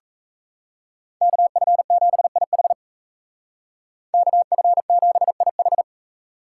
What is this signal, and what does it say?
Morse code, heard as a keyed tone, about 36 wpm, 700 Hz. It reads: KF7IH KF7IH